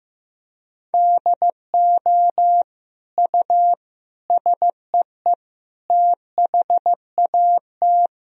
Morse code, heard as a keyed tone, about 15 wpm, 700 Hz. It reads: DO U SEE THAT
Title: Sentences from Top 500 Words and Abbreviations 15wpm